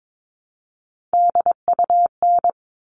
Morse code, heard as a keyed tone, about 22 wpm, 700 Hz. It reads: DUN
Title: All Cut Numbers - 3-Digit 22wpm